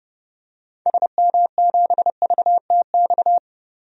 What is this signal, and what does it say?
Morse code, heard as a keyed tone, about 30 wpm, 700 Hz. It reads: SM7VTX
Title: DXCC-100 Call Signs - 6 Characters - 30wpm